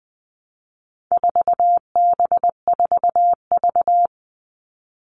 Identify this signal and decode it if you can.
Morse code, heard as a keyed tone, about 20 wpm, 700 Hz. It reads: 4B4V
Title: All Cut Numbers - 4-Digit 20wpm